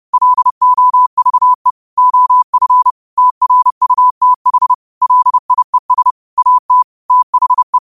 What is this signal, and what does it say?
Morse code, heard as a keyed tone, about 30 wpm, 1000 Hz. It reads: LOVE OF TRUTH LIES AT THE
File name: qotdfiles / 30wpm